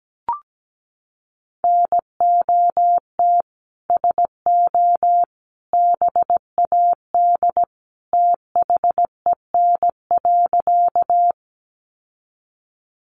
Morse code, heard as a keyed tone, about 17 wpm, 700 Hz. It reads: NOT SO BAD THEN.
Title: Sentences from Top 700 Words 17wpm